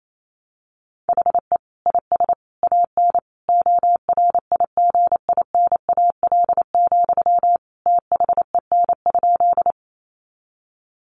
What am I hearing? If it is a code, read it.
Morse code, heard as a keyed tone, about 28 wpm, 700 Hz. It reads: HE IS AN ORIGINAL, THEN?
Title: Sentences from Top 900 Words 28wpm